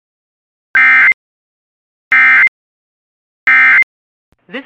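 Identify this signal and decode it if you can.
Emergency Alert System (SAME) tones, heard as harsh buzzing bursts. It is heard from 0.7 to 3.8 s.
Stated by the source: Alert tones from Weatherscan IntelliStar V1 weather channel system.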